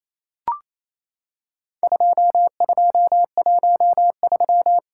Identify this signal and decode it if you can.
Morse code, heard as a keyed tone, about 28 wpm, 700 Hz. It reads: 2213
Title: Worked All Europe DX Contest - QTC Exchange - 28wpm